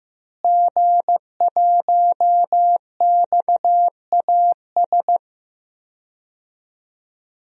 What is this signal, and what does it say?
Morse code, heard as a keyed tone, about 16 wpm, 700 Hz. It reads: G1XAS
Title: DXCC-100 Call Signs - 5 Characters - Repeated 3x - 15wpm